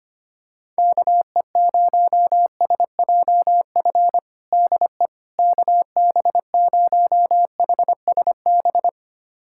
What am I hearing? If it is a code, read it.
Morse code, heard as a keyed tone, about 25 wpm, 700 Hz. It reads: KE0SJF DE KB0HSB
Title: USA Callsign DE Callsign - 6 Characters - 25wpm